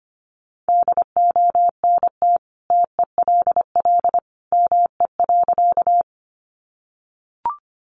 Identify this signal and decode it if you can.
Morse code, heard as a keyed tone, about 25 wpm, 700 Hz. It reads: DONT TELL ME.
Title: Sentences from Top 300 Words 25wpm